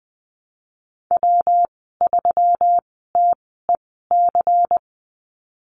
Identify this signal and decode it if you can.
Morse code, heard as a keyed tone, about 20 wpm, spaced as for 14 wpm, 700 Hz. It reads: W3TEC
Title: 200 Lesson 32g - Letter k - Call Signs 20wpm